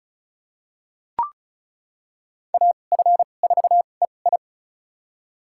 Morse code, heard as a keyed tone, about 36 wpm, spaced as for 25 wpm, 700 Hz. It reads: AF4EI
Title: USA Call Signs - 5 Characters - 2x Character Spacing - 35wpm